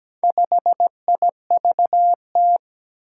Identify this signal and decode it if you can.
Morse code, heard as a keyed tone, about 17 wpm, 700 Hz. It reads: 5IVT
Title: Four Challenging Letter-Number Combinations 17wpm